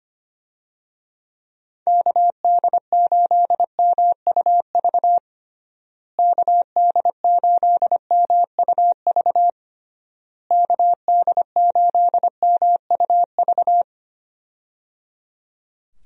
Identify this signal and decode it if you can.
Morse code, heard as a keyed tone, about 25 wpm, 700 Hz. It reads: KD8MUV KD8MUV KD8MUV
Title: USA Call Signs - 6 Characters - Repeated 3x - 25wpm